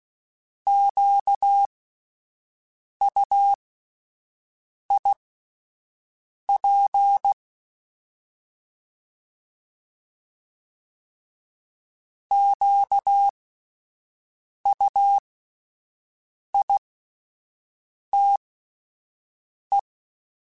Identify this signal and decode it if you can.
Morse code, heard as a keyed tone, about 16 wpm, spaced as for 5 wpm, 800 Hz. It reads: QUIP QUITE